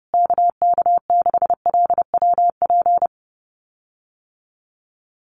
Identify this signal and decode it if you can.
Morse code, heard as a keyed tone, about 30 wpm, 700 Hz. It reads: KK6LWP
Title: USA Call Signs Repeated 2x 30wpm